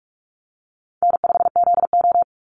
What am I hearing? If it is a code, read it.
Morse code, heard as a keyed tone, about 44 wpm, 700 Hz. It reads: N5ZO
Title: Call Signs Repeated 3x - 45wpm